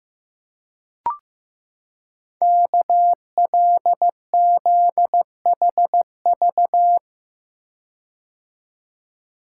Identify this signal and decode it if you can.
Morse code, heard as a keyed tone, about 15 wpm, 700 Hz. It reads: KLZHV
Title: Code Groups - 5 Letters - 15wpm